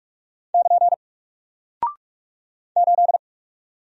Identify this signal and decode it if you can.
Morse code, heard as a keyed tone, about 44 wpm, 700 Hz. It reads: <KN> 8
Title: Single Character (Letter:Number:Punctuation:Prosign) 45wpm